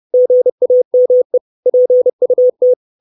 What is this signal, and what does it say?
Morse code, heard as a keyed tone, about 30 wpm, 500 Hz. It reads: GAME PUT
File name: 632 Words 30WPM